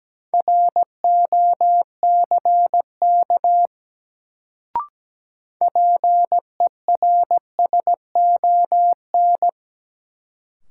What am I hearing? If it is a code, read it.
Morse code, heard as a keyed tone, about 17 wpm, 700 Hz. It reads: ROCK PERSON
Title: Sentences from Top 400 Words - New Word Drill - 17wpm